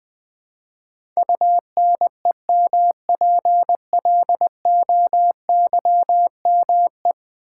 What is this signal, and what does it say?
Morse code, heard as a keyed tone, about 20 wpm, 700 Hz. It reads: UNEMPLOYME
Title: Build A Word - 5 Letters At A Time - 12-Letter Words - 20wpm